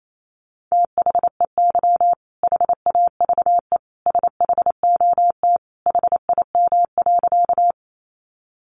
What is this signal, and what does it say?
Morse code, heard as a keyed tone, about 28 wpm, 700 Hz. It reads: THEY HAVE SHOT HIM.
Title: Sentences from Top 1100 Words 28wpm